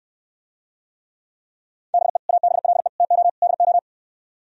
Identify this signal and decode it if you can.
Morse code, heard as a keyed tone, about 68 wpm, 700 Hz. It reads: BE ABLE TO DO